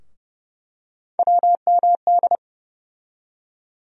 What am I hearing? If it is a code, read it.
Morse code, heard as a keyed tone, about 30 wpm, 700 Hz. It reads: WMD
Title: USA Call Sign Postfixes - 3 Letters 30wpm